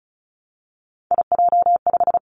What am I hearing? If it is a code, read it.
Morse code, heard as a keyed tone, about 36 wpm, 700 Hz. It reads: IJ5
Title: Three Challenging Letter-Number Combinations 35wpm